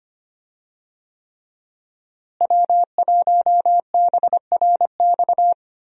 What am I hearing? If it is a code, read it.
Morse code, heard as a keyed tone, about 25 wpm, 700 Hz. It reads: W1BRX
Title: USA Call Signs - 5 Characters - Repeated 3x - 25wpm